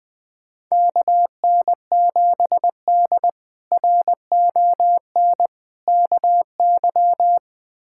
Morse code, heard as a keyed tone, about 20 wpm, 700 Hz. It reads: KN7D RON KY
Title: K1USN SST QSO Practice - 20wpm